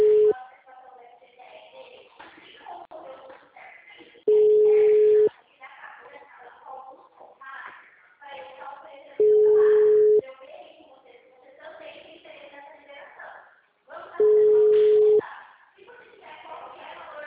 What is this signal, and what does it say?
Ringback tone (Europe, 425 Hz), with speech.